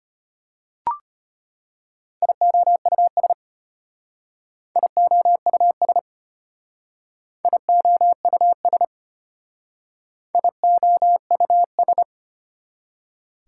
Morse code, heard as a keyed tone, about 31 wpm, 700 Hz. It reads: IOUS IOUS IOUS IOUS
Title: Speed-Racing Prefixes and Suffixes 25wpm